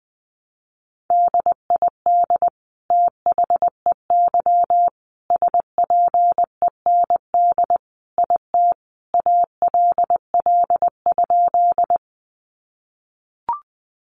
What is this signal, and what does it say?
Morse code, heard as a keyed tone, about 20 wpm, 700 Hz. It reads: DID THEY SPEND IT ALL?
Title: Sentences from Top 800 Words 20wpm